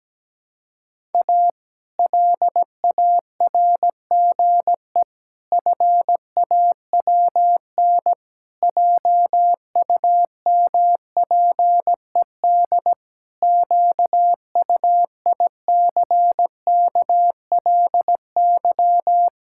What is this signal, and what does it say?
Morse code, heard as a keyed tone, about 17 wpm, 700 Hz. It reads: A LARGE FAWN JUMPED QUICKLY
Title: Pangram Challenge v2 - 17wpm